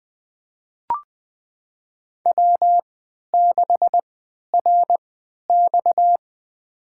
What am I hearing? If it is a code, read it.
Morse code, heard as a keyed tone, about 20 wpm, spaced as for 11 wpm, 700 Hz. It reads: W6RX
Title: USA Call Signs - 3x Character Spacing - 20wpm